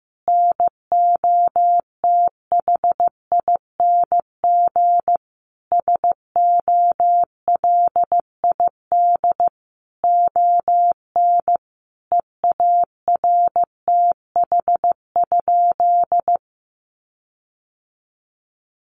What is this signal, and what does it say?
Morse code, heard as a keyed tone, about 15 wpm, 700 Hz. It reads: NOTHING SOLID ON EARTH?
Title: Sentences from Top 1500 Words 15wpm